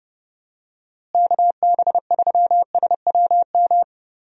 Morse code, heard as a keyed tone, about 31 wpm, 700 Hz. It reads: KB3SWM
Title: USA Call Signs - 6 Characters - Repeated 3x - 30wpm